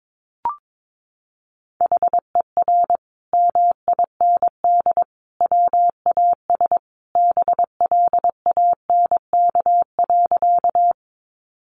Morse code, heard as a keyed tone, about 22 wpm, 700 Hz. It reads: HER MIND WAS BLANK.
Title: Sentences from Top 1500 Words 22wpm